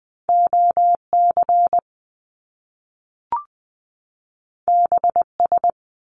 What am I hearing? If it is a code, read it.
Morse code, heard as a keyed tone, about 20 wpm, 700 Hz. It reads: OC BS
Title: Top Two Letter Combinations in Words 20wpm